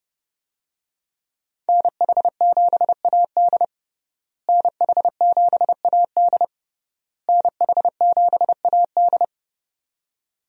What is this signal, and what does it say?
Morse code, heard as a keyed tone, about 30 wpm, 700 Hz. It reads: NH7AD NH7AD NH7AD
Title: USA Call Signs - 5 Characters - Repeated 3x - 30wpm